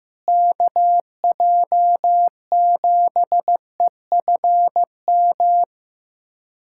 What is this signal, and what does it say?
Morse code, heard as a keyed tone, about 15 wpm, 700 Hz. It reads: KJ7EFM
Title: USA Callsign DE Callsign - 6 Characters - 15wpm